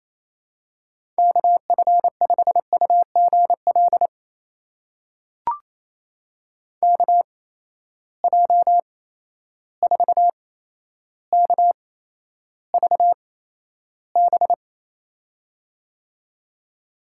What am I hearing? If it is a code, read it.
Morse code, heard as a keyed tone, about 28 wpm, spaced as for 12 wpm, 700 Hz. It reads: KF5UGL KJ4KVB
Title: USA Call Signs - 6 Characters - 8x Character Spacing - 28wpm